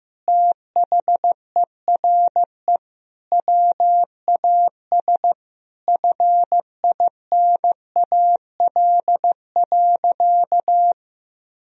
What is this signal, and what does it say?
Morse code, heard as a keyed tone, about 15 wpm, 700 Hz. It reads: THERE WAS FINAL.
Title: Sentences from Top 1000 Words - Review Part 2 - 15wpm